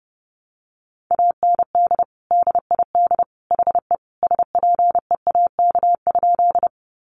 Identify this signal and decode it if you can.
Morse code, heard as a keyed tone, about 30 wpm, 700 Hz. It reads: AND DID HE SPEAK?
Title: Sentences from Top 1100 Words 30wpm